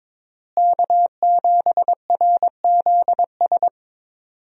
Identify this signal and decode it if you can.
Morse code, heard as a keyed tone, about 22 wpm, 700 Hz. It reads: K7RZS
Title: USA Call Signs - 5 Characters - Repeated 3x - 22wpm